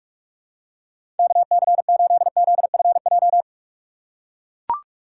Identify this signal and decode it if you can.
Morse code, heard as a keyed tone, about 44 wpm, 700 Hz. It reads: KC8ZFJ
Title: USA Call Signs - 2x Character Spacing - 45wpm